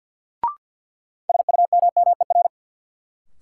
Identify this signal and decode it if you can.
Morse code, heard as a keyed tone, about 52 wpm, 700 Hz. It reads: SUMMER
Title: Sentences from Top 1100 Words - New Word Drill Alt - 50wpm